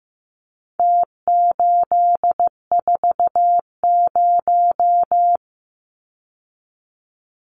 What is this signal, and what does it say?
Morse code, heard as a keyed tone, about 15 wpm, 700 Hz. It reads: T840
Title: Numbers - Contest Style - 4-Digit 15wpm